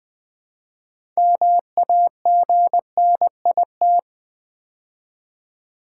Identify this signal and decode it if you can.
Morse code, heard as a keyed tone, about 20 wpm, 700 Hz. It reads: MAGNIT
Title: Build A Word - 3 Letters At A Time - 9-Letter Words - 20wpm